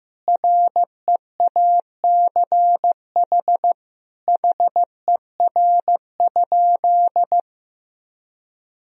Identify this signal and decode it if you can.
Morse code, heard as a keyed tone, about 15 wpm, 700 Hz. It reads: REACH HER?
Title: Sentences from Top 1000 Words - Review Part 2 - 15wpm